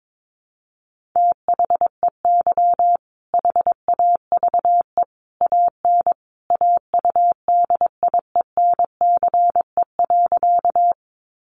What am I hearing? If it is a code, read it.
Morse code, heard as a keyed tone, about 22 wpm, 700 Hz. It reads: THEY HAVE AN AUDIENCE.
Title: Sentences from Top 800 Words 22wpm